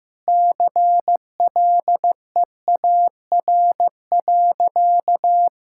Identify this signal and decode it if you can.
Morse code, heard as a keyed tone, about 16 wpm, 700 Hz. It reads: CLEAR.
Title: Sentences from Top 1600 Words 15wpm